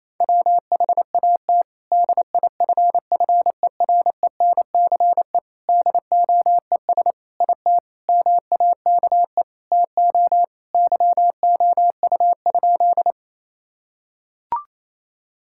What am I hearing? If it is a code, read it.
Morse code, heard as a keyed tone, about 28 wpm, 700 Hz. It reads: WHAT DIFFERENCE DOES IT MAKE TO YOU?
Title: Sentences from Top 1000 Words - Review Part 5 - 28wpm